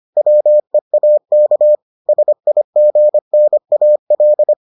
Morse code, heard as a keyed tone, about 25 wpm, 600 Hz. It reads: WEAK SIGNAL